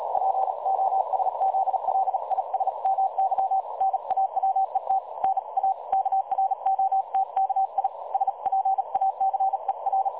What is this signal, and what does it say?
Morse code, heard as a keyed tone, about 38 wpm, 770 Hz. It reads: POSZLISMY NAWET NA KROTKI SPACER
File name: dlazaawansowanych / nagraniazpasma / websdr recording 3565.0kHz